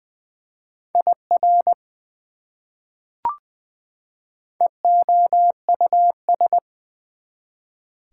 Morse code, heard as a keyed tone, about 20 wpm, 700 Hz. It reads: IR EOUS
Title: Prefixes and Suffixes 20wpm